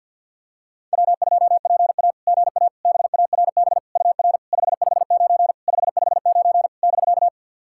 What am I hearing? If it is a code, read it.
Morse code, heard as a keyed tone, about 51 wpm, 700 Hz. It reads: W2PA GA BARB UR 559 559 <BK>
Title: SOTA QSO Practice 50wpm